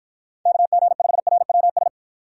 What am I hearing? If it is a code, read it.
Morse code, heard as a keyed tone, about 49 wpm, 700 Hz. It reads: KG5RWS